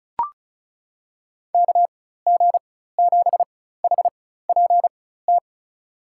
Morse code, heard as a keyed tone, about 36 wpm, spaced as for 16 wpm, 700 Hz. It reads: KG7HPT
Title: USA Call Signs - 4x Character Spacing - 35wpm